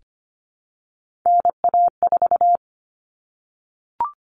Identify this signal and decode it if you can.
Morse code, heard as a keyed tone, about 25 wpm, 700 Hz. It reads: NA4
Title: All Cut Numbers - 3-Digit 25wpm